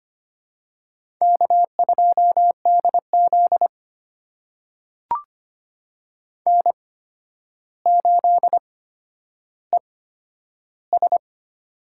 Morse code, heard as a keyed tone, about 25 wpm, spaced as for 11 wpm, 700 Hz. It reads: K2DZ N8ES
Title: USA Call Signs - 4 Characters - 8x Character Spacing - 25wpm